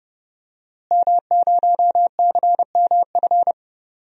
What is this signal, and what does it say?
Morse code, heard as a keyed tone, about 30 wpm, 700 Hz. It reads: M0CMF
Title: DXCC-100 Call Signs - 5 Characters - Repeated 2x - 30wpm